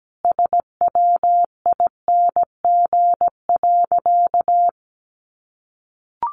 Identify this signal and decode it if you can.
Morse code, heard as a keyed tone, about 17 wpm, 700 Hz. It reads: SWING.
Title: Additional Easy Phrases 17wpm